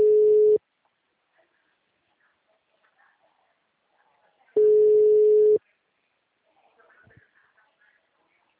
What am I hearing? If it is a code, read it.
Ringback tone (Europe, 425 Hz).